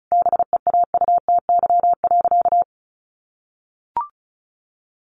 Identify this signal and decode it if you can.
Morse code, heard as a keyed tone, about 34 wpm, 700 Hz. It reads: BEAUTY.
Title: Sentences from Top 1200 Words 35wpm